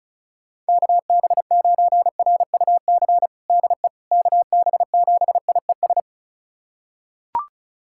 Morse code, heard as a keyed tone, about 36 wpm, 690 Hz. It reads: KB9RUC DE KB7IES
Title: USA Callsign DE Callsign - 6 Characters - 35wpm